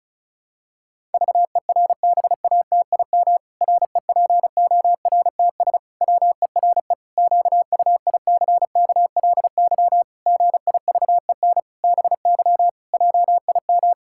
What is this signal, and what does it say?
Morse code, heard as a keyed tone, about 34 wpm, 700 Hz. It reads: VERBATIM REPORTS WERE QUICKLY GIVEN BY JIM
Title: Pangram Challenge v2 - 35wpm